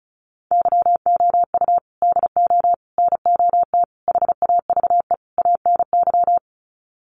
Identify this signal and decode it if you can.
Morse code, heard as a keyed tone, about 34 wpm, 700 Hz. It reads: YOU DO NOT HAVE ANY
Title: Sets of 5 Words Encore 35wpm